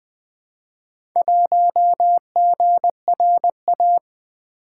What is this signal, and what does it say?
Morse code, heard as a keyed tone, about 20 wpm, 700 Hz. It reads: 1GRA
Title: Four Letter-Number Combinations - 20wpm